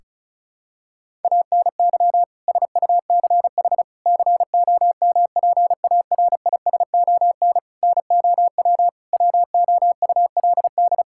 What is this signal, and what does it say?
Morse code, heard as a keyed tone, about 34 wpm, 700 Hz. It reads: ANY SUCH COMPARISON NOW WOULD
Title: Sentences from Top 1600 Words 35wpm